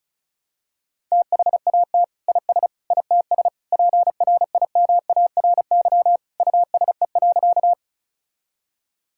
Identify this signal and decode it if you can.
Morse code, heard as a keyed tone, about 36 wpm, 700 Hz. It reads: THAT IS ITS PRIMARY USE.
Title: Sentences from Top 1000 Words - Review Part 5 - 35wpm